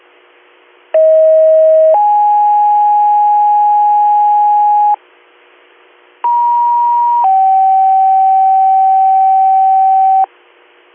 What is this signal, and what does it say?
Two-tone page, heard as a one-second tone, then a three-second tone, about 640 then 830 Hz, about 950 then 770 Hz. It is heard from 0.9 to 10.2 s.